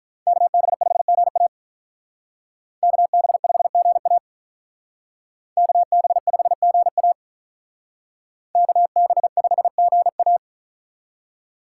Morse code, heard as a keyed tone, about 46 wpm, 700 Hz. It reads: KB5GA KB5GA KB5GA KB5GA